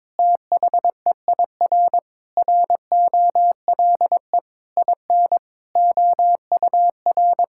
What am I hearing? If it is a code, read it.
Morse code, heard as a keyed tone, about 22 wpm, 700 Hz. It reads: THEIR ROLE IN OUR
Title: Sentences from Top 1000 Words - Review Part 6 - 22wpm